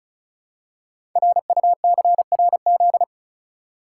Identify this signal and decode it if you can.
Morse code, heard as a keyed tone, about 36 wpm, 690 Hz. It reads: RUCRZ
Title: Code Groups - 5 Letters - 35wpm